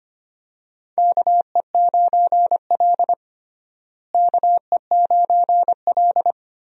Morse code, heard as a keyed tone, about 25 wpm, 700 Hz. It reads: KE9L KE9L
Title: USA Call Signs - 4 Characters - Repeated 3x - 25wpm